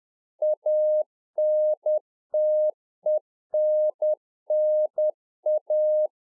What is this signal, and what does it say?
Morse code, heard as a keyed tone, about 10 wpm, 610 Hz. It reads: ANTENNA